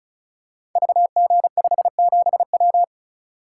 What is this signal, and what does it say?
Morse code, heard as a keyed tone, about 36 wpm, 700 Hz. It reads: VG57W
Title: Code Groups - 5 Letter:Numbers - 35wpm